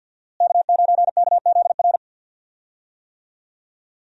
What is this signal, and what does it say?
Morse code, heard as a keyed tone, about 53 wpm, 700 Hz. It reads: K9KZR
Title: USA Call Signs - 5 Characters - Repeated 2x - 50wpm